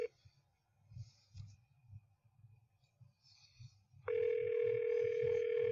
Ringback tone (North America), 440 + 480 Hz.